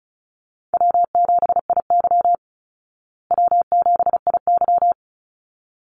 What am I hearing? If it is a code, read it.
Morse code, heard as a keyed tone, about 34 wpm, 700 Hz. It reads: W7IY W7IY
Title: Easy Call Signs Repeated 2x - 35wpm